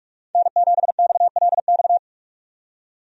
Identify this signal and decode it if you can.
Morse code, heard as a keyed tone, about 46 wpm, 700 Hz. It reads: N7XLX
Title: USA Call Signs - 5 Characters - 45wpm